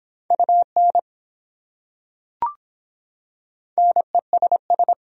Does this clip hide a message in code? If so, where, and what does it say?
Yes: Morse code from 0.3 to 4.9 s, heard as a keyed tone, about 23 wpm, 700 Hz. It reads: UN NESS